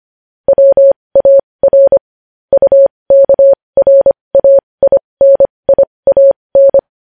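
Morse code, heard as a keyed tone, about 25 wpm, spaced as for 20 wpm, 550 Hz. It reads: WAR UKRAINIAN
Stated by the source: Morse code news headlines sent at 20.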